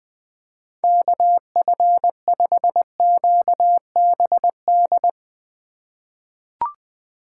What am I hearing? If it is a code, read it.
Morse code, heard as a keyed tone, about 20 wpm, 700 Hz. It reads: KF5QBD
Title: USA Call Signs - 6 Characters - 3x Character Spacing - 20wpm